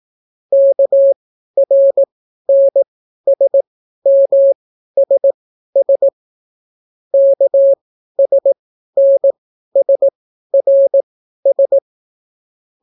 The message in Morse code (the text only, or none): KRNSMSS KSNSRS